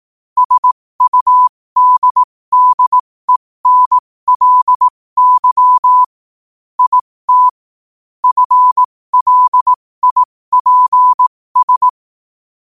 Morse code, heard as a keyed tone, about 18 wpm, spaced as for 15 wpm, 1000 Hz. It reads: SUDDENLY IT FLIPS
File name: qotdfiles / 15wpm